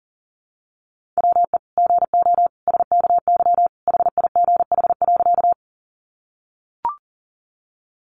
Morse code, heard as a keyed tone, about 37 wpm, 700 Hz. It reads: WE GO SKY HIGH.
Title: Sentences from Top 1200 Words 40wpm